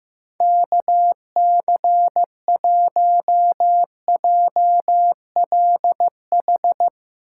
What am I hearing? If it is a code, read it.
Morse code, heard as a keyed tone, about 15 wpm, 700 Hz. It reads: KC1JLH
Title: USA Call Signs - 6 Characters - 8x Character Spacing - 15wpm